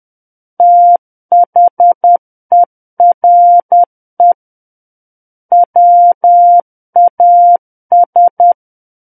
Morse code, heard as a keyed tone, about 10 wpm, 700 Hz. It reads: THERE WAS
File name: Book One / 01-Chapter 1 - 10 WPM